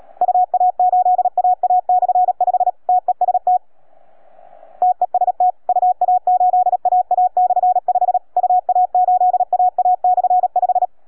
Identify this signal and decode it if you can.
Morse code, heard as a keyed tone, about 37 wpm, 720 Hz. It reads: UA8AA/5 TEST TEST UA8AA/5 UA8AA/5